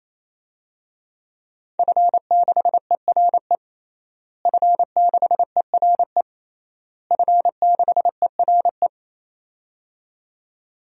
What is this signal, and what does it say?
Morse code, heard as a keyed tone, about 27 wpm, 700 Hz. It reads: F6ERE F6ERE F6ERE
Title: DXCC-100 Call Signs - 5 Characters - Repeated 3x - 28wpm